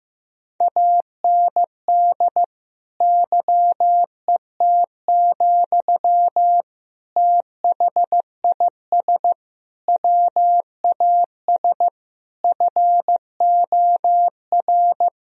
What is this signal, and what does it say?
Morse code, heard as a keyed tone, about 16 wpm, 700 Hz. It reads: AND YET, THIS WAS FOR
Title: Sentences from Top 1000 Words - Review Part 3 - 15wpm